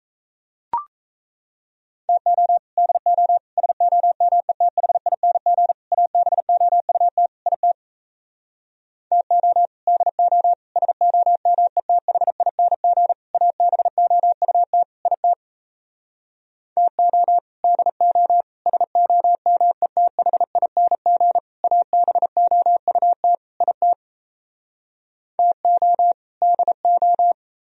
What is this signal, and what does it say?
Morse code, heard as a keyed tone, about 38 wpm, 700 Hz. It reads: TO DO SOMETHING ABOUT IT TO DO SOMETHING ABOUT IT TO DO SOMETHING ABOUT IT TO DO